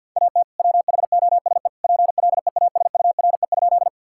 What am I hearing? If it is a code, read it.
Morse code, heard as a keyed tone, about 50 wpm, 700 Hz. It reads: AT WHOSE PLEASURE?
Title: Sentences from Top 1600 Words 50wpm